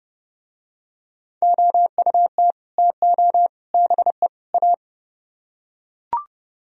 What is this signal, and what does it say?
Morse code, heard as a keyed tone, about 28 wpm, 700 Hz. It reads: OUT TO BE A